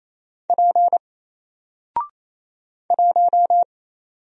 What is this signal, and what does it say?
Morse code, heard as a keyed tone, about 27 wpm, 700 Hz. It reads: P 1
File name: Single-Letter-Number-28wpm